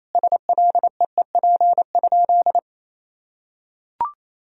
Morse code, heard as a keyed tone, about 28 wpm, 700 Hz. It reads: SLEEP?
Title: Sentences from Top 500 Words and Abbreviations 28wpm